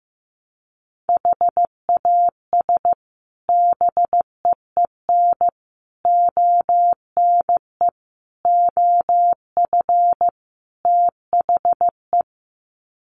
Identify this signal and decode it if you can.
Morse code, heard as a keyed tone, about 15 wpm, 700 Hz. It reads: HAS BEEN ONE OF THE
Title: Sets of 5 Words Encore 15wpm